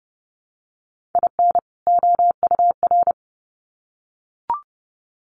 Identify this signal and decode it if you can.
Morse code, heard as a keyed tone, about 30 wpm, 700 Hz. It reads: IN OUR
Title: Sets of 2 Words 30wpm